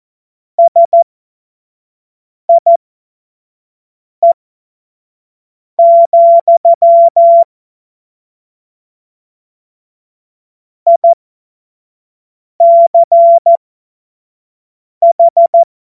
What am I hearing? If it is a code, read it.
Morse code, heard as a keyed tone, about 14 wpm, spaced as for 5 wpm, 680 Hz. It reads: SIE, ICH
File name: Morsekurs / Witze / Witz 04. Januar 2022, 25 F70